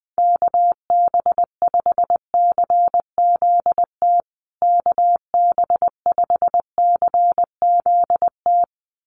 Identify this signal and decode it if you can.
Morse code, heard as a keyed tone, about 20 wpm, 700 Hz. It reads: KB5CZT KB5CZT
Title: USA Call Signs Repeated 2x 20wpm